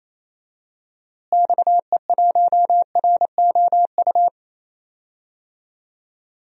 Morse code, heard as a keyed tone, about 28 wpm, 700 Hz. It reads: XE1ROU
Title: DXCC-100 Call Signs - 6 Characters - Repeated 3x - 28wpm